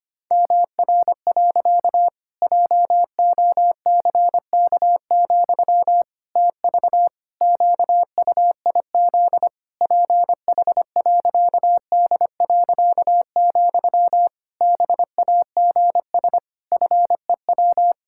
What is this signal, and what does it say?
Morse code, heard as a keyed tone, about 25 wpm, 700 Hz. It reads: MR. JOCK, TV QUIZ PH.D., BAGS FEW
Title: Pangram Challenge v2 - 25wpm